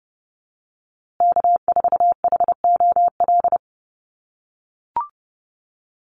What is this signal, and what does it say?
Morse code, heard as a keyed tone, about 30 wpm, 700 Hz. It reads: K4HOL
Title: USA Call Signs Repeated 3x 30wpm